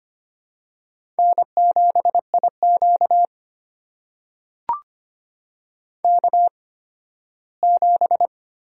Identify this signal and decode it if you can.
Morse code, heard as a keyed tone, about 25 wpm, spaced as for 15 wpm, 700 Hz. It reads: N7IQ K7
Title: USA Call Signs - 4 Characters - 8x Character Spacing - 25wpm